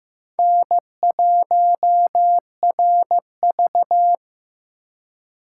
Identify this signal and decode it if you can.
Morse code, heard as a keyed tone, about 18 wpm, 700 Hz. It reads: N1RV
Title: Speed Racing - USA 1x2 Call Signs 15wpm